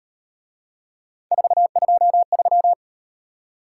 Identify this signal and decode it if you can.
Morse code, heard as a keyed tone, about 37 wpm, 700 Hz. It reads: 423